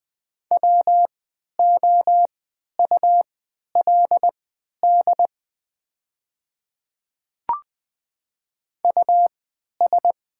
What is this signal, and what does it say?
Morse code, heard as a keyed tone, about 20 wpm, spaced as for 11 wpm, 690 Hz. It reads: WOULD US
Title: 3x Character Spacing - Top 300 Words - 20wpm